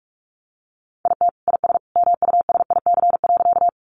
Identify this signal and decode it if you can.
Morse code, heard as a keyed tone, about 44 wpm, 700 Hz. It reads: IT IS MUSIC.
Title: Sentences from Top 300 Words 45wpm